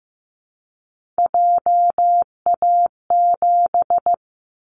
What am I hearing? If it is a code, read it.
Morse code, heard as a keyed tone, about 15 wpm, 700 Hz. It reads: JA7
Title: Callsign Prefixes for Top 50 DXCC - 15wpm